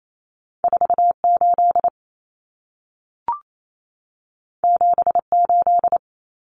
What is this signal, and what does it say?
Morse code, heard as a keyed tone, about 28 wpm, 700 Hz. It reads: 48 78
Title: Common Cut Numbers - 2-Digit 28wpm